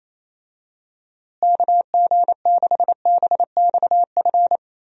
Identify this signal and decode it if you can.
Morse code, heard as a keyed tone, about 28 wpm, 700 Hz. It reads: KG6BXF
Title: USA Call Signs - 6 Characters - Repeated 3x - 28wpm